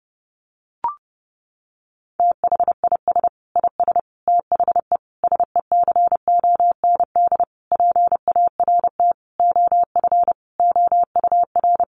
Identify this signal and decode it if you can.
Morse code, heard as a keyed tone, about 30 wpm, 700 Hz. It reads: THIS IS THE SECOND PART OF OUR
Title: Sentences from Top 1400 Words 30wpm